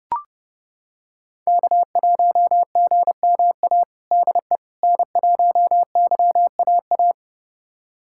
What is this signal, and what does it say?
Morse code, heard as a keyed tone, about 30 wpm, 700 Hz. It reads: K1GMA DE N1YAA
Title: USA Callsign DE Callsign - 5 Characters - 30wpm